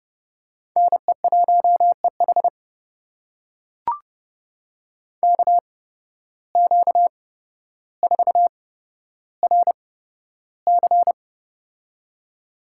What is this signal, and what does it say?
Morse code, heard as a keyed tone, about 30 wpm, spaced as for 13 wpm, 700 Hz. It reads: NE1EH KQ4RC
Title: USA Call Signs - 5 Characters - 8x Character Spacing - 30wpm